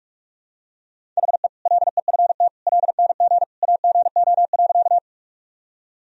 Morse code, heard as a keyed tone, about 41 wpm, 700 Hz. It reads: HE LEFT LNG AGO.
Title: Sentences from Top 500 Words and Abbreviations 45wpm